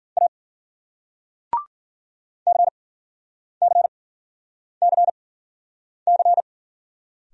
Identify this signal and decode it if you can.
Morse code, heard as a keyed tone, about 50 wpm, spaced as for 9 wpm, 700 Hz. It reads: A CCCC